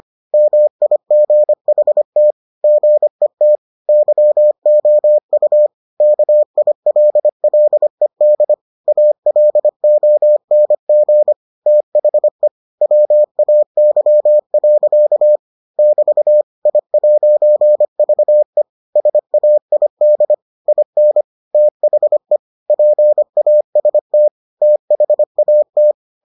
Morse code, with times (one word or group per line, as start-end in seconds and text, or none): 0.3-2.3 MIGHT
2.6-3.6 GET
3.9-5.7 YOU
6.0-8.5 KILLED
8.9-11.3 ALONG
11.7-12.5 THE
12.8-15.4 WAY.
15.8-16.4 <BT>
16.7-18.6 I'VE
19.0-20.3 SAID
20.7-21.2 IN
21.5-22.4 THE
22.7-24.3 PAST
24.6-25.9 THAT